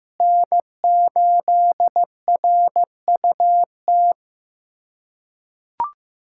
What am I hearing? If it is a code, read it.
Morse code, heard as a keyed tone, about 15 wpm, 700 Hz. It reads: N8RUT
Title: USA Callsign DE Callsign - 5 Characters - 15wpm